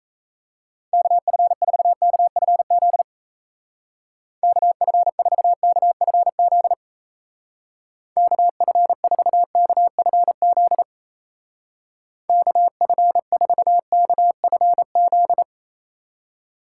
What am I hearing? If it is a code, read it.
Morse code, heard as a keyed tone, about 35 wpm, 700 Hz. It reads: KF4KFZ KF4KFZ KF4KFZ KF4KFZ
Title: Speed Racing - USA 2x3 Call Signs 28wpm